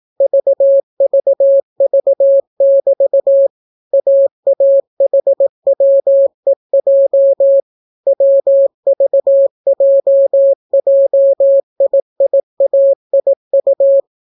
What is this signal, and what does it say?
Morse code, heard as a keyed tone, about 18 wpm, 550 Hz. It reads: VVV<BT> AAHWEJ WVJJIIAIU